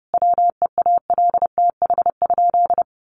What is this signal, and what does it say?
Morse code, heard as a keyed tone, about 30 wpm, 700 Hz. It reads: WEALTH?
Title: Sentences from Top 1200 Words 30wpm